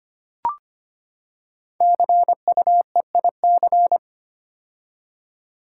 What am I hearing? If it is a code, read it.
Morse code, heard as a keyed tone, about 25 wpm, 700 Hz. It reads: CUEIC
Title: Code Groups - 5 Letter:Numbers - 25wpm